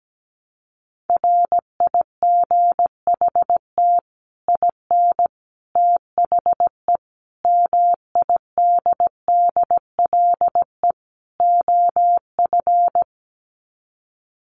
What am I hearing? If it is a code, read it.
Morse code, heard as a keyed tone, about 17 wpm, 700 Hz. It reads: RIGHT IN THE MIDDLE OF
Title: Sets of 5 Words 17wpm